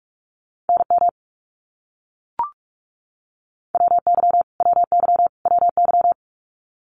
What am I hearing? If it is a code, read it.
Morse code, heard as a keyed tone, about 44 wpm, 700 Hz. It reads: NM WY WY WY
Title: US States Repeated 3x 45wpm